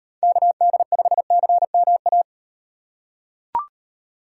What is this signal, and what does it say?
Morse code, heard as a keyed tone, about 36 wpm, 700 Hz. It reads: KD5CMA